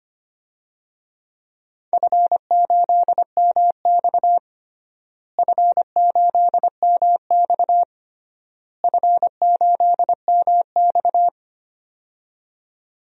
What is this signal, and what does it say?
Morse code, heard as a keyed tone, about 25 wpm, 700 Hz. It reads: F8MX F8MX F8MX
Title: DXCC-100 Call Signs - 4 Characters - Repeated 3x - 25wpm